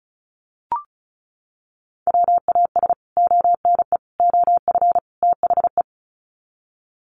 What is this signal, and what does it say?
Morse code, heard as a keyed tone, about 34 wpm, 700 Hz. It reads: WAS ONE OF THE
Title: Sets of 4 Words 35wpm